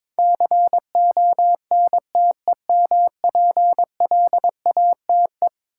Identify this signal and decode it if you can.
Morse code, heard as a keyed tone, about 22 wpm, 700 Hz. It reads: CONTEMPLATE
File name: practice / 22wpm / Eleven Letter Words - 22wpm